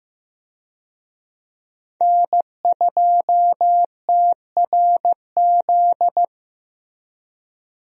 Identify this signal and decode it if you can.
Morse code, heard as a keyed tone, about 15 wpm, 700 Hz. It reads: N2TRZ
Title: USA Call Signs - 5 Characters - Repeated 2x - 15wpm